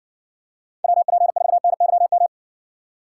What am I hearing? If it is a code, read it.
Morse code, heard as a keyed tone, about 58 wpm, 700 Hz. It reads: WP4NYM